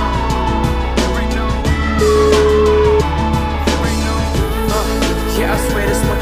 Ringback tone (Europe, 425 Hz), with noise.